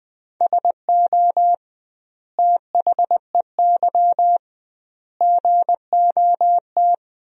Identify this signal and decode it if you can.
Morse code, heard as a keyed tone, about 20 wpm, 700 Hz. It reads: SO THEY GOT
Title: 2x Word-Spacing - Sentences from Top 300 Words - 20wpm